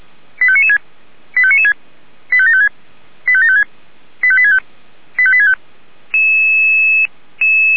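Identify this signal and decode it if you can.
Five-tone selective call (ZVEI), heard as quick runs of five tones, tones of about 70 ms. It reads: 85996 85996 85756 85756 85865 85865
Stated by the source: ZVEI five-tone selective call received by SDR.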